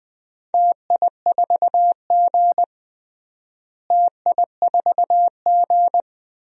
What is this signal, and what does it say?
Morse code, heard as a keyed tone, about 20 wpm, 700 Hz. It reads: TI4G TI4G
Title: DXCC-100 Call Signs - 4 Characters - Repeated 3x - 20wpm